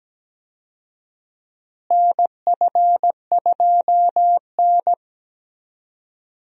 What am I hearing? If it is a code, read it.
Morse code, heard as a keyed tone, about 17 wpm, 700 Hz. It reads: NF2N
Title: USA Call Signs - 4 Characters - Repeated 3x - 17wpm